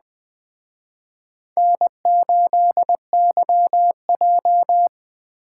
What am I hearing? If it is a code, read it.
Morse code, heard as a keyed tone, about 20 wpm, 700 Hz. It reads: N8YJ
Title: Speed Racing - USA 1x2 Call Signs 17wpm